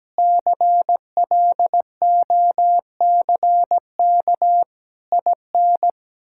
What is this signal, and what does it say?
Morse code, heard as a keyed tone, about 17 wpm, 700 Hz. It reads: CLOCK IN
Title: Pangram Challenge v2 - 17wpm